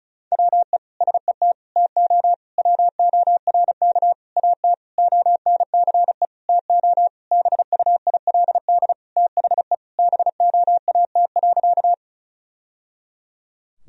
Morse code, heard as a keyed tone, about 36 wpm, 700 Hz. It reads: WE SET TO WORK AT ONCE TO BUILD THE BOAT.
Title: Sentences from Top 1000 Words - Review Part 1 - 35wpm